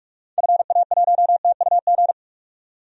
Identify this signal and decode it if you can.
Morse code, heard as a keyed tone, about 42 wpm, 700 Hz. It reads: FA1TUG
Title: DXCC-100 Call Signs - 6 Characters - 45wpm